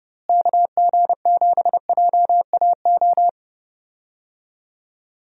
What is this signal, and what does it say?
Morse code, heard as a keyed tone, about 30 wpm, 700 Hz. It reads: KG7JAO
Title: USA Call Signs - 6 Characters - Repeated 3x - 30wpm